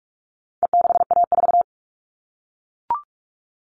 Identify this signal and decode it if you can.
Morse code, heard as a keyed tone, about 44 wpm, 700 Hz. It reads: E6A4
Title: All Cut Numbers - 4-Digit 45wpm